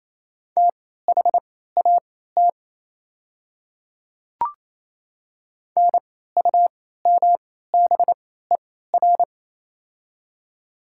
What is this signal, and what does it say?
Morse code, heard as a keyed tone, about 28 wpm, spaced as for 16 wpm, 700 Hz. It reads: THAT NUMBER
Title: 3x Character Spacing - Top 200 Words - 28wpm